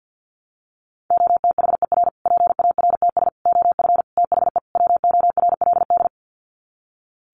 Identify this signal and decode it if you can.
Morse code, heard as a keyed tone, about 50 wpm, 700 Hz. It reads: OTHER PARTS OF THE WORLD
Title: Sets of 5 Words Encore 50wpm